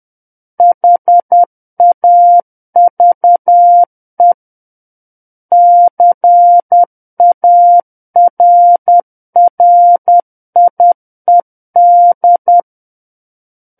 Morse code, heard as a keyed tone, about 10 wpm, 700 Hz. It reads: HAVE CARRIED
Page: War of the Worlds - 01-Chapter 1 - 10 WPM